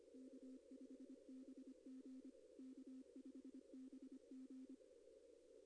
Morse code, heard as a keyed tone, about 26 wpm, 260 Hz. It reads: K5BG K5BG
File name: contests / arrl-160-k5tr-op-w5tn -2200